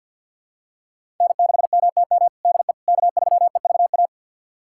Morse code, heard as a keyed tone, about 48 wpm, 700 Hz. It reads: N6MTM DE K3EVA
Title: USA Callsign DE Callsign - 5 Characters - 50wpm